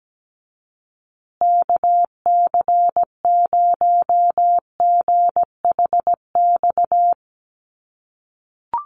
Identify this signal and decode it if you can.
Morse code, heard as a keyed tone, about 17 wpm, 700 Hz. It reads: KC0GHX
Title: USA Call Signs Repeated 3x 17wpm